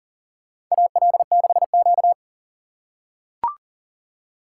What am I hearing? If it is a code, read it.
Morse code, heard as a keyed tone, about 40 wpm, 690 Hz. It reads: AL6Q
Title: USA Call Signs - 4 Characters - 2x Character Spacing - 40wpm